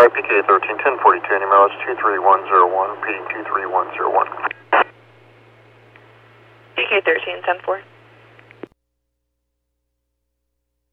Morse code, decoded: none

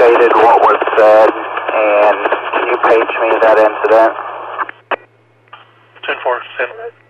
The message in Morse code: none

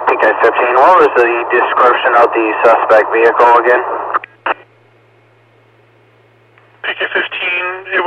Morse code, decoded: none

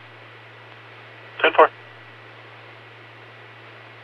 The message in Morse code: none

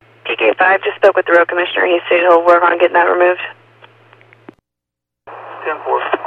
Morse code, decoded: none

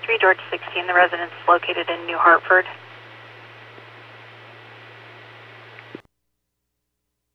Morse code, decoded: none

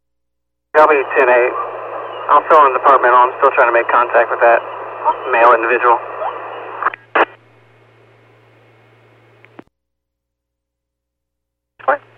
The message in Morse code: none

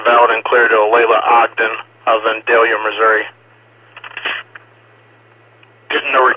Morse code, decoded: none

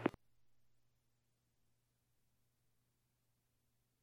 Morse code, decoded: none

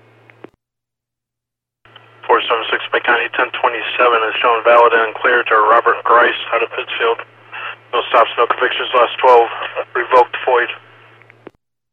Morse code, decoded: none

